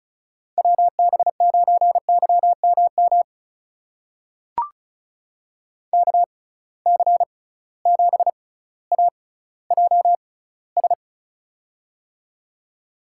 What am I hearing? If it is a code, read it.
Morse code, heard as a keyed tone, about 36 wpm, spaced as for 18 wpm, 700 Hz. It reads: WB9YMM KC7AJS
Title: USA Call Signs - 6 Characters - 6x Character Spacing - 35wpm